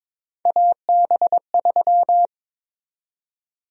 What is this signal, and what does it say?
Morse code, heard as a keyed tone, about 22 wpm, 700 Hz. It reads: AB3
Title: USA Call Sign Prefixes - Repeated 2x - 22wpm